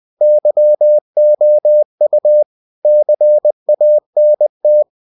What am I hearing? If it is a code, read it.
Morse code, heard as a keyed tone, about 20 wpm, 600 Hz. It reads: YOU CANT
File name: cw-books / winnie-the-pooh / winnie-20wpm-page-0030